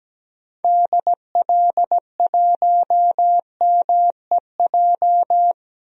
Morse code, heard as a keyed tone, about 17 wpm, 700 Hz. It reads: DL1MEJ
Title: DXCC-100 Call Signs - 6 Characters - 17wpm